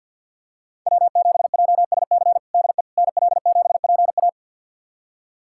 Morse code, heard as a keyed tone, about 51 wpm, 700 Hz. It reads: W7JSC DE NL7PA
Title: USA Callsign DE Callsign - 5 Characters - 50wpm